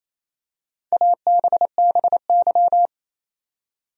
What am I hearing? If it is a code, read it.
Morse code, heard as a keyed tone, about 28 wpm, 700 Hz. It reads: ABBY
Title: USA Common Names - Max 5 Letters - 28wpm